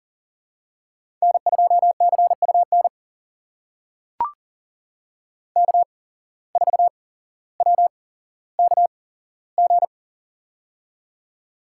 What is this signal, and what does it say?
Morse code, heard as a keyed tone, about 42 wpm, spaced as for 17 wpm, 700 Hz. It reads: N2CUN K4WKG
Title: USA Call Signs - 5 Characters - 8x Character Spacing - 40wpm